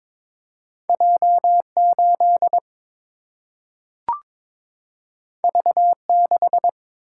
Morse code, heard as a keyed tone, about 22 wpm, 700 Hz. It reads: J 8 V 6
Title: Single-Letter Single-Number 22wpm